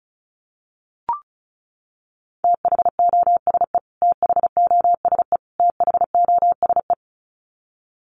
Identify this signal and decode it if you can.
Morse code, heard as a keyed tone, about 34 wpm, 700 Hz. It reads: THOSE THOSE THOSE
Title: Top 500 Words Repeated 35wpm